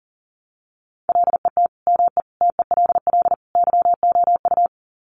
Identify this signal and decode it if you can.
Morse code, heard as a keyed tone, about 40 wpm, 700 Hz. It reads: LET ME TELL YOU
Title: Sets of 4 Words 40wpm